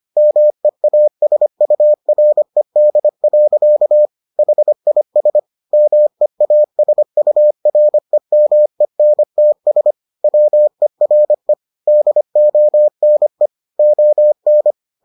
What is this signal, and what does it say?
Morse code, heard as a keyed tone, about 25 wpm, 600 Hz. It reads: MEASURED. HIS MEASUREMENTS WERE DONE ON